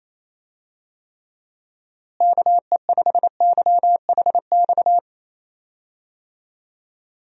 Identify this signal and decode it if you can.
Morse code, heard as a keyed tone, about 27 wpm, 700 Hz. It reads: KE5YHX
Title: USA Call Signs - 6 Characters - Repeated 3x - 28wpm